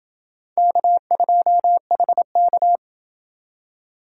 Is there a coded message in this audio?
Yes: Morse code from 0.6 to 2.8 s, heard as a keyed tone, about 28 wpm, 700 Hz. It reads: K2HK